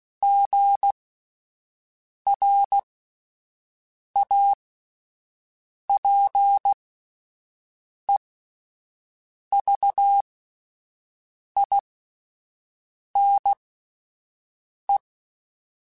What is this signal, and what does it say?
Morse code, heard as a keyed tone, about 16 wpm, spaced as for 5 wpm, 800 Hz. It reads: GRAPEVINE